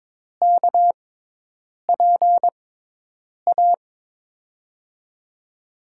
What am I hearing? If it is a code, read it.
Morse code, heard as a keyed tone, about 22 wpm, spaced as for 8 wpm, 700 Hz. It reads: KPA
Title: USA Call Signs - 6x Character Spacing - 22wpm